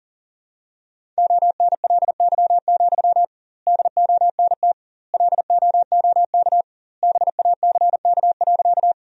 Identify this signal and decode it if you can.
Morse code, heard as a keyed tone, about 37 wpm, 700 Hz. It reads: ONLY, DONT LOOK BACK.
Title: 2x Word-Spacing - Sentences from Top 300 Words - 40wpm